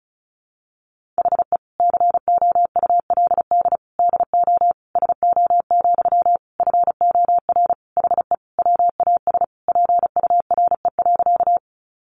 Morse code, heard as a keyed tone, about 34 wpm, 700 Hz. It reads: HE COULD DO SO, FOR HE WAS PURE.
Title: Sentences from Top 1500 Words 35wpm